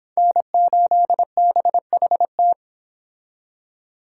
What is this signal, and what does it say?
Morse code, heard as a keyed tone, about 26 wpm, 700 Hz. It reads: N8BHT